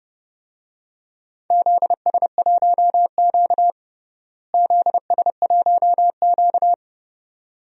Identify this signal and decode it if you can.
Morse code, heard as a keyed tone, about 30 wpm, 700 Hz. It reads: ZS1Q ZS1Q
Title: DXCC-100 Call Signs - 4 Characters - Repeated 2x - 30wpm